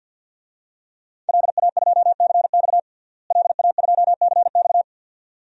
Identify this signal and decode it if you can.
Morse code, heard as a keyed tone, about 46 wpm, 700 Hz. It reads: LA2CX LA2CX
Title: DXCC-100 Call Signs - 5 Characters - Repeated 2x - 50wpm